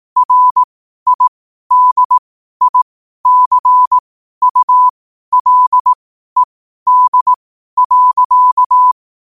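Morse code, heard as a keyed tone, about 18 wpm, spaced as for 12 wpm, 1000 Hz. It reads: RIDICULED.
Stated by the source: Morse code quote of the day at 12 wpm.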